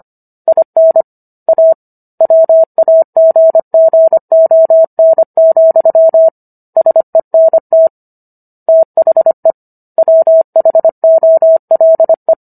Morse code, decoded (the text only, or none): IN A WAGGON, SENT THE WHOLE